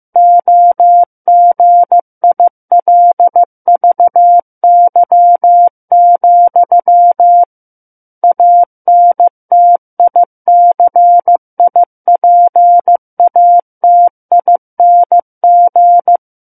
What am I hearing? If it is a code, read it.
Morse code, heard as a keyed tone, about 15 wpm, 700 Hz. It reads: OGILVY, ANTICIPATING
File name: Book One / 06-Chapter 6 - 15 WPM